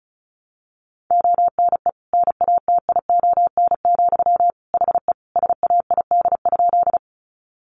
Morse code, heard as a keyed tone, about 34 wpm, 700 Hz. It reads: ONE NATION, HE SAID?
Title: Sentences from Top 900 Words 35wpm